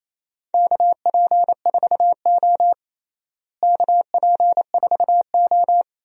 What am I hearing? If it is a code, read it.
Morse code, heard as a keyed tone, about 28 wpm, 700 Hz. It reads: KP4O KP4O
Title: DXCC-100 Call Signs - 4 Characters - Repeated 3x - 28wpm